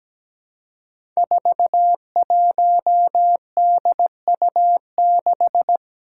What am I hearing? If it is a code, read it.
Morse code, heard as a keyed tone, about 17 wpm, 700 Hz. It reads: 41DU6
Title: Five Challenging Letter-Number Combinations 17wpm